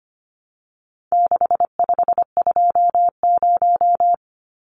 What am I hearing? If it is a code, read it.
Morse code, heard as a keyed tone, about 25 wpm, 700 Hz. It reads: 6520
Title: Numbers - 4-Digit 25wpm